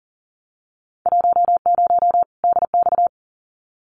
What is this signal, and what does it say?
Morse code, heard as a keyed tone, about 40 wpm, 700 Hz. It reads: 10 DX
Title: Winter Field Day Exchanges 40wpm